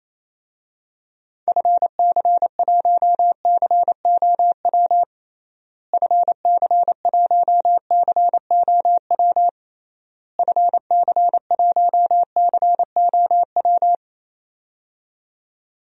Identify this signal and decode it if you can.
Morse code, heard as a keyed tone, about 28 wpm, 700 Hz. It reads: FC1COW FC1COW FC1COW
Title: DXCC-100 Call Signs - 6 Characters - Repeated 3x - 28wpm